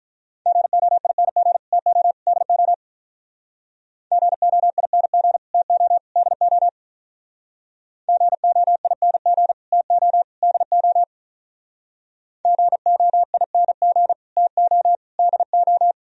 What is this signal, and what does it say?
Morse code, heard as a keyed tone, about 45 wpm, 700 Hz. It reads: GOING TO DO GOING TO DO GOING TO DO GOING TO DO